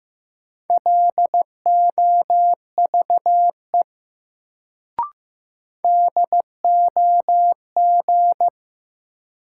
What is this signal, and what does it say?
Morse code, heard as a keyed tone, about 15 wpm, 700 Hz. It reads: LOVE DOG
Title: Sentences from Top 400 Words - New Word Drill - 15wpm